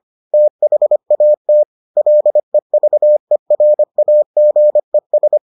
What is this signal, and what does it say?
Morse code, heard as a keyed tone, about 25 wpm, 600 Hz. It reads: THAT LEVERAGES